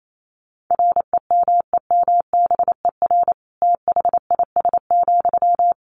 Morse code, heard as a keyed tone, about 28 wpm, 700 Hz. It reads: REMEMBER THIS,
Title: Sentences from Top 1500 Words 28wpm